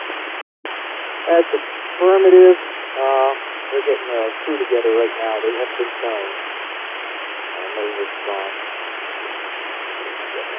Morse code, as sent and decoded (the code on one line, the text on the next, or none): none